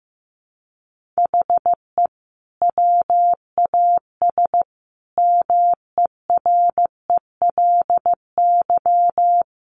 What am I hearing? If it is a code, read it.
Morse code, heard as a keyed tone, about 15 wpm, 700 Hz. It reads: HE WAS MERELY
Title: Sentences from Top 1400 Words 15wpm